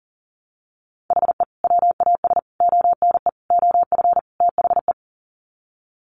Morse code, heard as a keyed tone, about 40 wpm, 700 Hz. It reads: HE WAS ONE OF THE
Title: Sets of 5 Words 40wpm